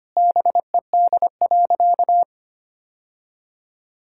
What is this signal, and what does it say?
Morse code, heard as a keyed tone, about 25 wpm, 700 Hz. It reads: BED.
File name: practice / 25wpm / Sentences with 3-Word Phrases - 25wpm